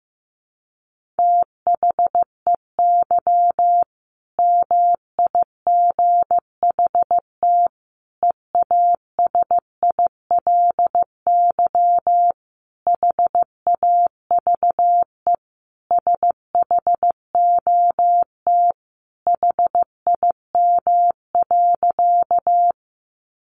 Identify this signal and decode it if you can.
Morse code, heard as a keyed tone, about 15 wpm, 700 Hz. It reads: THEY MIGHT EASILY HAVE SHOT HIM.
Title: Sentences from Top 1100 Words 15wpm